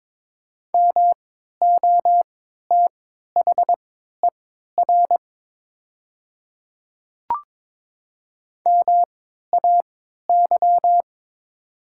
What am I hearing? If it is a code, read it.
Morse code, heard as a keyed tone, about 22 wpm, spaced as for 12 wpm, 700 Hz. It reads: MOTHER MAY
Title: 3x Character Spacing - Top 200 Words - 22wpm